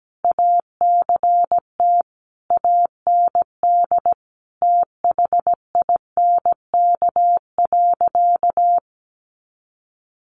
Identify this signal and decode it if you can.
Morse code, heard as a keyed tone, about 17 wpm, 700 Hz. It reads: ACT AND THINK.
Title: Sentences from Top 500 Words 17wpm